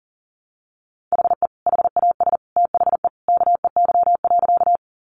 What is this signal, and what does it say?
Morse code, heard as a keyed tone, about 40 wpm, 700 Hz. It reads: HE HAS THE KEY.
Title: Sentences from Top 600 Words 40wpm